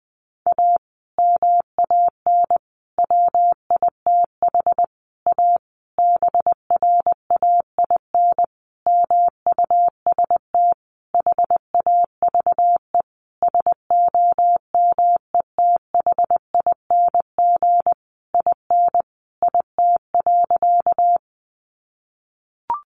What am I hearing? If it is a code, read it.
Morse code, heard as a keyed tone, about 20 wpm, 700 Hz. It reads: A MAN WITH A BRAIN MUST HAVE SOMETHING IN IT.
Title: Sentences from Top 1100 Words 20wpm